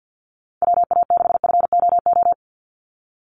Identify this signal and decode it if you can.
Morse code, heard as a keyed tone, about 50 wpm, 700 Hz. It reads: WA6FOO
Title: USA Call Signs 50wpm